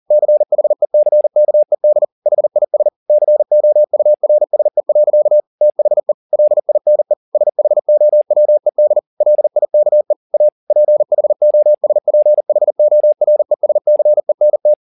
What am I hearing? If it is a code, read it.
Morse code, heard as a keyed tone, about 40 wpm, 600 Hz. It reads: CHECKED HIS COURSE. THE LINE SHOWED LIKE A PHOSPHORESCENT